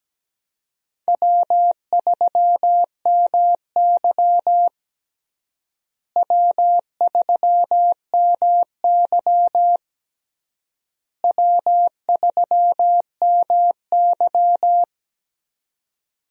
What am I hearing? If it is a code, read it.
Morse code, heard as a keyed tone, about 17 wpm, 700 Hz. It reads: W3MY W3MY W3MY
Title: USA Call Signs - 4 Characters - Repeated 3x - 17wpm